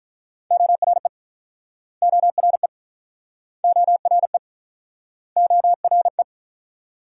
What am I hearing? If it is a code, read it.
Morse code, heard as a keyed tone, about 49 wpm, 700 Hz. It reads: ORE ORE ORE ORE